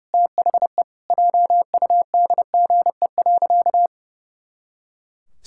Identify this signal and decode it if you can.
Morse code, heard as a keyed tone, about 32 wpm, 700 Hz. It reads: THE JUDGE.
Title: Pangram Challenge v2 - 30wpm